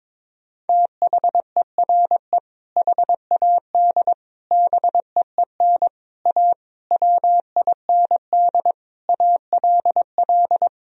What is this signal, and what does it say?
Morse code, heard as a keyed tone, about 22 wpm, 700 Hz. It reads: THERE HAD BEEN A WIND ALL
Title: Sentences from Top 1000 Words - Review Part 2 - 22wpm